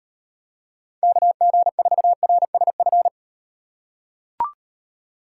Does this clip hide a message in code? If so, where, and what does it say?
Yes: Morse code from 1.0 to 3.1 s, heard as a keyed tone, about 38 wpm, 700 Hz. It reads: KG4RSF